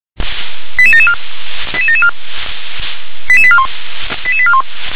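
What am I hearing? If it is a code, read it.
Five-tone selective call (ZVEI), heard as quick runs of five tones, tones of about 70 ms. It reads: 88774 88774 88841 88841
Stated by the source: ZVEI five-tone selective call 88910 received on SDR.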